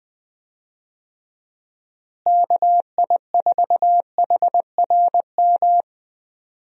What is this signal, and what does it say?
Morse code, heard as a keyed tone, about 20 wpm, 700 Hz. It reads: KI4HRM
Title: USA Call Signs - 6 Characters - Repeated 3x - 20wpm